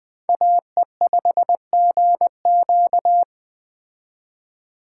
Morse code, heard as a keyed tone, about 20 wpm, 700 Hz. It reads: AE5GQ
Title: Speed Racing - USA 2x2 Call Signs 17wpm